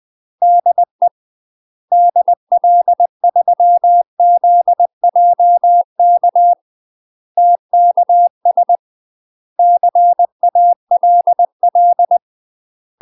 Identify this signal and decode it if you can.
Morse code, heard as a keyed tone, about 20 wpm, 700 Hz. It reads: DE DL3ZJK TKS CALL
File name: 20WPM